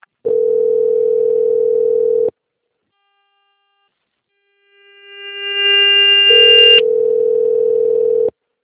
Ringback tone (North America), 440 + 480 Hz, with noise.